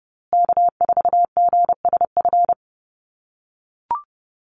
Morse code, heard as a keyed tone, about 30 wpm, 700 Hz. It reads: K4GSF
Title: USA Call Signs Repeated 2x 30wpm